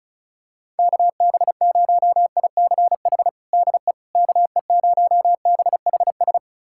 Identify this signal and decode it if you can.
Morse code, heard as a keyed tone, about 36 wpm, 700 Hz. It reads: KB0ICH DE KE0BHS
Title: USA Callsign DE Callsign - 6 Characters - 35wpm